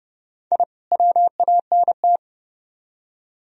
Morse code, heard as a keyed tone, about 40 wpm, 700 Hz. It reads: I WANT